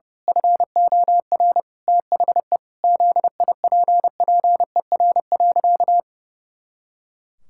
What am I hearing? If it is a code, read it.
Morse code, heard as a keyed tone, about 30 wpm, 700 Hz. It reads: FOR THE ZIPPER.
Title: Pangram Challenge v2 - 30wpm